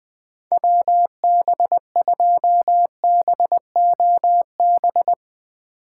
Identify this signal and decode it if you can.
Morse code, heard as a keyed tone, about 21 wpm, 700 Hz. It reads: WB2BOB
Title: Speed Racing - USA 2x3 Call Signs 20wpm